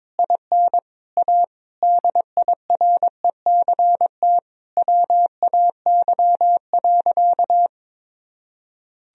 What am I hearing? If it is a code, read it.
Morse code, heard as a keyed tone, about 22 wpm, 700 Hz. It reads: IN A DIRECT WAY.
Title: Sentences from Top 1000 Words - Review Part 2 - 22wpm